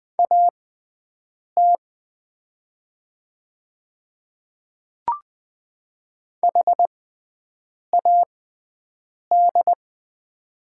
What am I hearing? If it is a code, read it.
Morse code, heard as a keyed tone, about 20 wpm, spaced as for 7 wpm, 700 Hz. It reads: AT HAD
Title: 6x Character Spacing - Top 300 Words - 20wpm